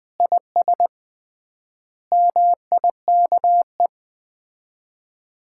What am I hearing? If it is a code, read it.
Morse code, heard as a keyed tone, about 20 wpm, 700 Hz. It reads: IS MIKE
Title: Common QSO Phrases - 3x Word-Spacing - 20wpm